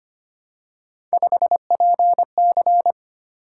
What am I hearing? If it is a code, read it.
Morse code, heard as a keyed tone, about 25 wpm, 700 Hz. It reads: 5PC
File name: Three Letter-Number Combinations - 25wpm